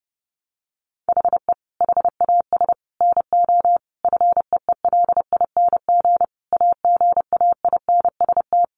Morse code, heard as a keyed tone, about 30 wpm, 700 Hz. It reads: HE HAS NO FEELING AGAINST
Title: Sentences from Top 1200 Words 30wpm